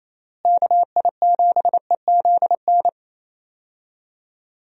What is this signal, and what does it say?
Morse code, heard as a keyed tone, about 28 wpm, 700 Hz. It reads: KI7EZN
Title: USA Call Signs - 6 Characters - Repeated 2x - 28wpm